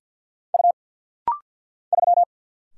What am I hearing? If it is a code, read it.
Morse code, heard as a keyed tone, about 50 wpm, 690 Hz. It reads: U 3
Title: Single Letter-Number Alt - Rapid-Fire 50wpm